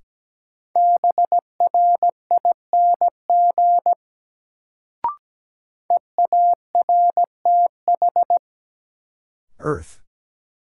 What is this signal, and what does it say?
Morse code, heard as a keyed tone, about 17 wpm, 700 Hz. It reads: BRING EARTH
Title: Sets of 5 Words Encore - New Word Drill - 17wpm